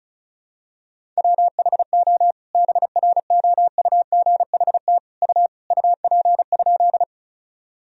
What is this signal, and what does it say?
Morse code, heard as a keyed tone, about 36 wpm, 700 Hz. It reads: WHO BROUGHT U UP?
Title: Sentences from Top 500 Words and Abbreviations 35wpm